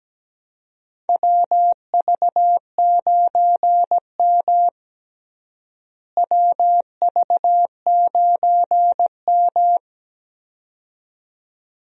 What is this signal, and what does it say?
Morse code, heard as a keyed tone, about 17 wpm, 700 Hz. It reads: WV9M WV9M
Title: USA Call Signs - 4 Characters - Repeated 2x - 17wpm